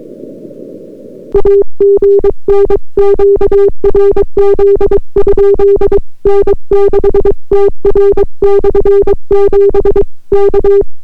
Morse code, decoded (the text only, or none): AGN QRZ? N6TR/7 K